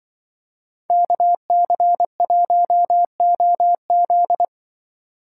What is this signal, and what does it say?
Morse code, heard as a keyed tone, about 24 wpm, 700 Hz. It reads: KC1OZ
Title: Speed Racing - USA 2x2 Call Signs 20wpm